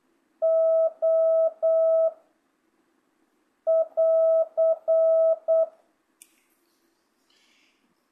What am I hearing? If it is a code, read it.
Morse code, heard as a keyed tone, about 8 wpm, 660 Hz. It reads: O <AR>
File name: jh8cbh / 2025cw